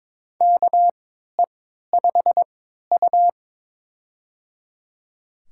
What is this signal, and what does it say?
Morse code, heard as a keyed tone, about 22 wpm, spaced as for 13 wpm, 700 Hz. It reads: KE5U
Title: USA Call Signs - 4 Characters - 3x Character Spacing - 22wpm